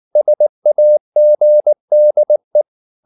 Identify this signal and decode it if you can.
Morse code, heard as a keyed tone, about 19 wpm, 600 Hz. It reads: SAGDE
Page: Ferie 19 wpm